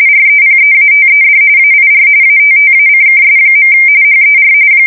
RTTY, about 45.45 baud. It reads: REPEATED AIR ATTACKS. HER
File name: fsk170 / USS-New-Jersey